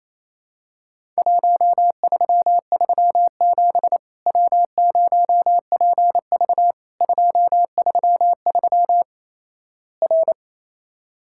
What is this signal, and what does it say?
Morse code, heard as a keyed tone, about 28 wpm, 700 Hz. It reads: 1337 W0PV 233 R
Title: Worked All Europe DX Contest - QTC Exchange - 28wpm